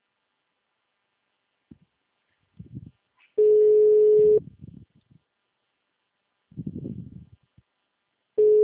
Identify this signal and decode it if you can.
Ringback tone (Europe, 425 Hz).